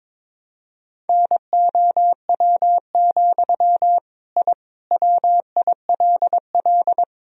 Morse code, heard as a keyed tone, about 22 wpm, 700 Hz. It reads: NOW, I WILL
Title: Sentences from Top 1000 Words - Review Part 3 - 22wpm